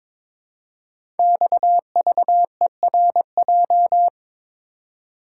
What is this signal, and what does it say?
Morse code, heard as a keyed tone, about 23 wpm, 700 Hz. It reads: XVERJ
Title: Code Groups - 5 Letters - 22wpm